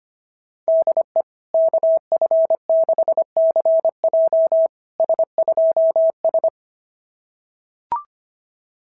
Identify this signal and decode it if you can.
Morse code, heard as a keyed tone, about 24 wpm, 650 Hz. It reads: DE KF6CJ S2S
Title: SOTA QSO Practice 25wpm